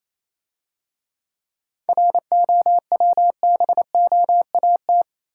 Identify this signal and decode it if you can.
Morse code, heard as a keyed tone, about 28 wpm, 690 Hz. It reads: ROWBOAT